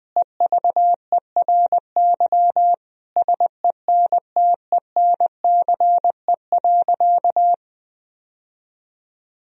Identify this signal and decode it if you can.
Morse code, heard as a keyed tone, about 21 wpm, 700 Hz. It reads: EVERY SENTENCE.
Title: Sentences from Top 1000 Words - Review Part 1 - 20wpm